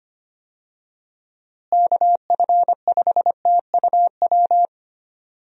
Morse code, heard as a keyed tone, about 25 wpm, 700 Hz. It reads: KF5TUW
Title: USA Call Signs - 6 Characters - Repeated 3x - 25wpm